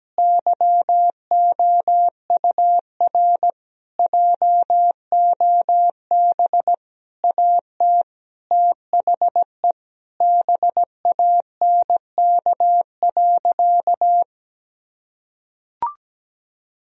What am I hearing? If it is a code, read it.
Morse code, heard as a keyed tone, about 17 wpm, 700 Hz. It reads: YOUR JOB AT THE BANK.
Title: Sentences from Top 1000 Words - Review Part 4 - 17wpm